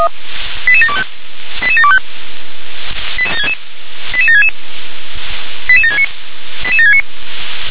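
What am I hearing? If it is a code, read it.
Five-tone selective call (ZVEI), heard as quick runs of five tones, tones of about 70 ms. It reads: 88725 88725 88766 88769 88769
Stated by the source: Five-tone ZVEI selective call, code 88724, received by SDR.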